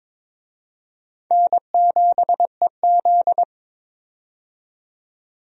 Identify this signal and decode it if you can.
Morse code, heard as a keyed tone, about 22 wpm, 700 Hz. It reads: N7EZ
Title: USA Call Signs - 4 Characters - 22wpm